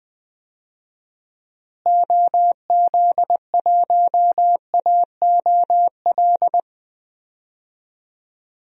Morse code, heard as a keyed tone, about 20 wpm, 700 Hz. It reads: OZ1AOL
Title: DXCC-100 Call Signs - 6 Characters - Repeated 2x - 20wpm